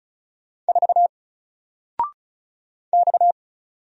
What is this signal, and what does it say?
Morse code, heard as a keyed tone, about 36 wpm, 700 Hz. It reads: 4 X
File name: Single-Letter-Number-35wpm